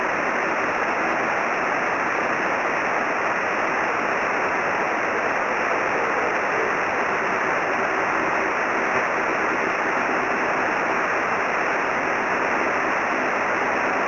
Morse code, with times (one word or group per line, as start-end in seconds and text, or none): none